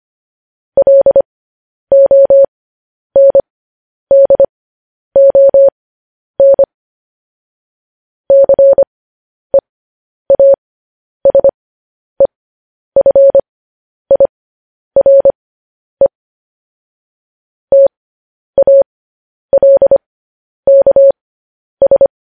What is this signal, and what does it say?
Morse code, heard as a keyed tone, about 25 wpm, spaced as for 10 wpm, 550 Hz. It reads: LONDON CEASEFIRE TALKS
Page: news headlines in morse code at 10 wpm